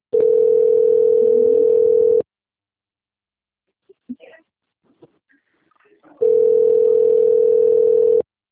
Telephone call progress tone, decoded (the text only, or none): ringback tone